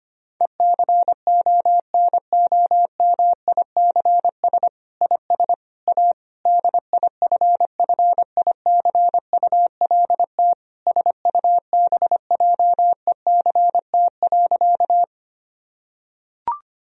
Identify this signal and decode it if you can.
Morse code, heard as a keyed tone, about 25 wpm, 700 Hz. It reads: ECONOMICS IS A DIFFICULT SUBJECT.
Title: Sentences from Top 1000 Words - Review Part 3 - 25wpm